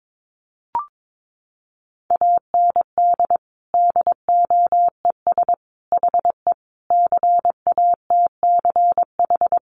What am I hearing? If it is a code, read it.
Morse code, heard as a keyed tone, about 22 wpm, 700 Hz. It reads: AND DOES HE CATCH
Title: Sentences from Top 1500 Words 22wpm